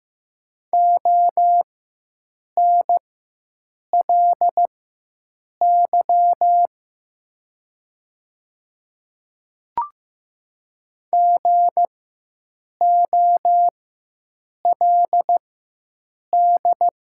Morse code, heard as a keyed tone, about 15 wpm, spaced as for 7 wpm, 690 Hz. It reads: ONLY GOLD
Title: 4x Character Spacing - Top 500 Words - 15wpm